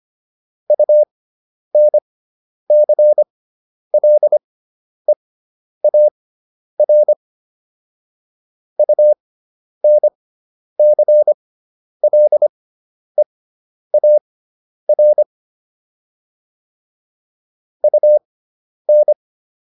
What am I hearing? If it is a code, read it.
Morse code, heard as a keyed tone, about 25 wpm, spaced as for 10 wpm, 600 Hz. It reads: UNCLEAR UNCLEAR UN